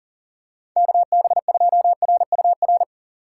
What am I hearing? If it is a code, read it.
Morse code, heard as a keyed tone, about 40 wpm, 700 Hz. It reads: KB2RUR
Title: USA Call Signs - 6 Characters - 2x Character Spacing - 40wpm